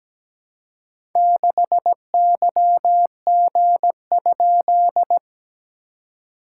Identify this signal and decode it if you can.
Morse code, heard as a keyed tone, about 17 wpm, 700 Hz. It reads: 6YG?
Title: SOTA QSO Practice 17wpm